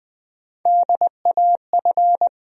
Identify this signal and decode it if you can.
Morse code, heard as a keyed tone, about 20 wpm, 700 Hz. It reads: DAF
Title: USA Call Sign Postfixes - 3 Letters 20wpm